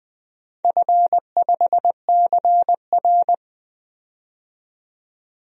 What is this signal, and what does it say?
Morse code, heard as a keyed tone, about 20 wpm, 700 Hz. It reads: F5CR
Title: DXCC-100 Call Signs - 4 Characters - 20wpm